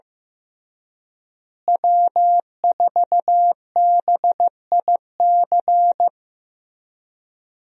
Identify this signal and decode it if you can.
Morse code, heard as a keyed tone, about 15 wpm, 700 Hz. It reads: W4BIC
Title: USA Call Signs - 5 Characters - Repeated 3x - 15wpm